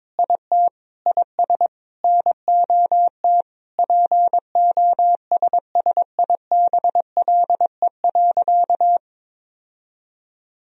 Morse code, heard as a keyed tone, about 22 wpm, 700 Hz. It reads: IT IS NOT POSSIBLE.
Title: Sentences from Top 1000 Words - Review Part 1 - 22wpm